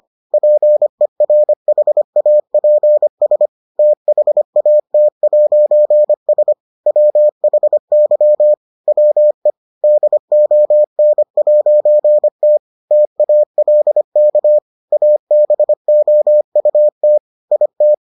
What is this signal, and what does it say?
Morse code, heard as a keyed tone, about 25 wpm, 600 Hz. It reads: PERHAPS THAT'S WHY WE DON'T TALK ABOUT IT